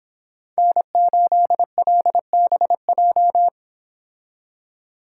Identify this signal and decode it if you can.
Morse code, heard as a keyed tone, about 24 wpm, 700 Hz. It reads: N8LBJ